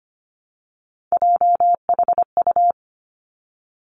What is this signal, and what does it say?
Morse code, heard as a keyed tone, about 25 wpm, 700 Hz. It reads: JHU
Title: Three Challenging Letter-Number Combinations 25wpm